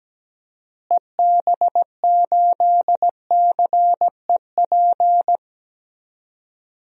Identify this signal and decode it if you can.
Morse code, heard as a keyed tone, about 17 wpm, 700 Hz. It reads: EB8CEP
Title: DXCC-100 Call Signs - 6 Characters - Repeated 3x - 17wpm